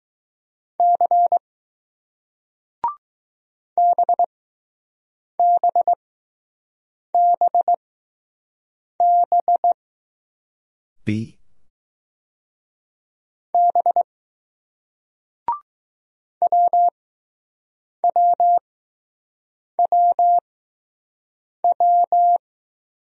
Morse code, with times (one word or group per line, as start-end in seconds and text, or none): 0.8-1.4 C
3.8-9.7 BBBB
13.5-14.0 B
16.4-22.4 WWWW